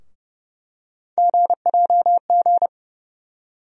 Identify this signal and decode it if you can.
Morse code, heard as a keyed tone, about 32 wpm, 700 Hz. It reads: GJG
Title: USA Call Sign Postfixes - 3 Letters 30wpm